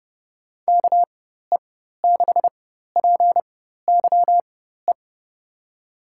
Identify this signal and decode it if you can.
Morse code, heard as a keyed tone, about 30 wpm, spaced as for 14 wpm, 700 Hz. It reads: KE6PYE
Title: USA Call Signs - 6 Characters - 4x Character Spacing - 30wpm